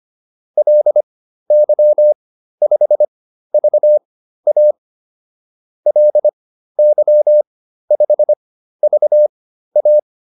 Morse code, heard as a keyed tone, about 25 wpm, spaced as for 13 wpm, 600 Hz. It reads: LY5VA LY5VA